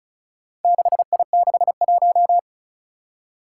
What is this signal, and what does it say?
Morse code, heard as a keyed tone, about 36 wpm, 700 Hz. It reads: 6I61
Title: Four Challenging Letter-Number Combinations 35wpm